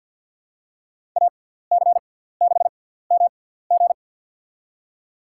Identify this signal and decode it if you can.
Morse code, heard as a keyed tone, about 53 wpm, spaced as for 17 wpm, 700 Hz. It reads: AC6MG
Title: USA Call Signs - 5 Characters - 6x Character Spacing - 50wpm